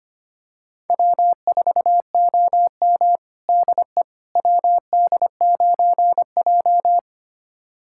Morse code, heard as a keyed tone, about 25 wpm, 700 Hz. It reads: W4OM DE WD9J
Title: USA Callsign DE Callsign - 4 Characters - 25wpm